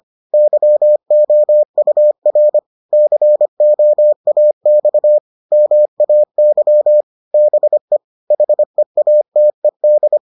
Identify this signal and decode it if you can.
Morse code, heard as a keyed tone, about 25 wpm, 600 Hz. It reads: YOUR COAX MAY BE HEATED